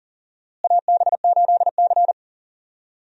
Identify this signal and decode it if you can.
Morse code, heard as a keyed tone, about 40 wpm, 700 Hz. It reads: AB8C
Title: USA Call Signs - 4 Characters - Repeated 2x - 40wpm